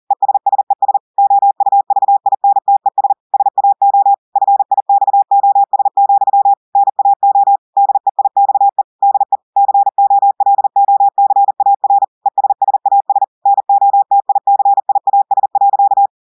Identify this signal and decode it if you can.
Morse code, heard as a keyed tone, about 40 wpm, 800 Hz. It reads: ESSES OUVINTES SAO FIXOS, NAO DEIXE DE COLOCAR ESSAS NOTICIAS.
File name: audios / portugais / portugais texte en clair 2 vitesse 40 mots minutes